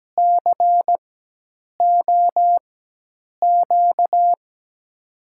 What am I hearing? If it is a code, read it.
Morse code, heard as a keyed tone, about 17 wpm, spaced as for 8 wpm, 700 Hz. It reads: COQ
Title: USA Call Signs - 6 Characters - 4x Character Spacing - 17wpm